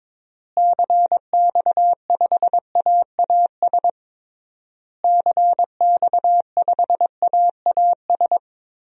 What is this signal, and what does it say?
Morse code, heard as a keyed tone, about 22 wpm, 700 Hz. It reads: CX5AAS CX5AAS
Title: DXCC-100 Call Signs - 6 Characters - Repeated 3x - 22wpm